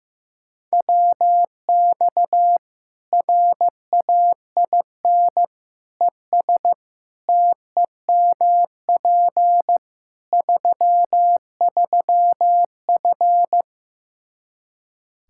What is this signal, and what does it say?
Morse code, heard as a keyed tone, about 15 wpm, 700 Hz. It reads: WX RAIN ES TEMP 33F
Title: Practice QSO - Exchange 2 - Info - 15wpm